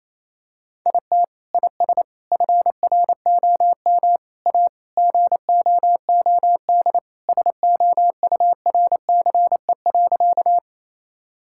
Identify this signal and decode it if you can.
Morse code, heard as a keyed tone, about 28 wpm, 700 Hz. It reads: IT IS FROM A GOOD SOURCE.
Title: Sentences from Top 1000 Words - Review Part 5 - 28wpm